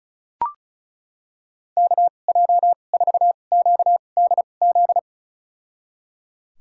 Morse code, heard as a keyed tone, about 34 wpm, spaced as for 25 wpm, 700 Hz. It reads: KJ4QDZ
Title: USA Call Signs - 6 Characters - 2x Character Spacing - 35wpm